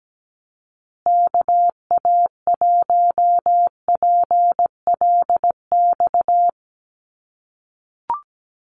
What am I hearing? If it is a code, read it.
Morse code, heard as a keyed tone, about 17 wpm, 700 Hz. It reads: KA1PLX
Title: USA Call Signs Repeated 3x 17wpm